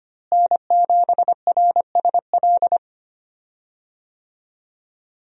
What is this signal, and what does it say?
Morse code, heard as a keyed tone, about 25 wpm, 700 Hz. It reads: N7RSL
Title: USA Call Signs - 5 Characters - Repeated 3x - 25wpm